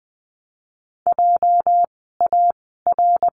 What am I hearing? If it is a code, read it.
Morse code, heard as a keyed tone, about 20 wpm, 700 Hz. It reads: J A R
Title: 206 Lesson 33f - Letter j - Words 20wpm